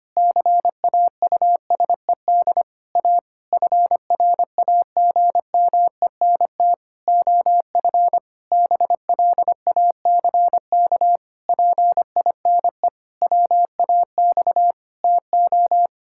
Morse code, heard as a keyed tone, about 25 wpm, 700 Hz. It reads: CAUSED A FRAGMENT OF BLACK PINE WAX TO
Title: Pangram Challenge v2 - 25wpm